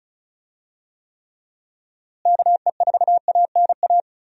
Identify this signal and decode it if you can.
Morse code, heard as a keyed tone, about 36 wpm, 700 Hz. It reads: KE4ANA
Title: USA Call Signs - 6 Characters - Repeated 3x - 35wpm